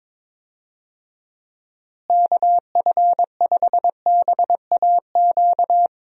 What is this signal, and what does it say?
Morse code, heard as a keyed tone, about 22 wpm, 700 Hz. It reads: KF5BAQ
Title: USA Call Signs - 6 Characters - Repeated 2x - 22wpm